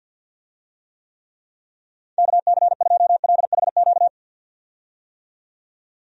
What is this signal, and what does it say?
Morse code, heard as a keyed tone, about 49 wpm, 700 Hz. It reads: KC2LHQ
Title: USA Call Signs - 6 Characters - 50wpm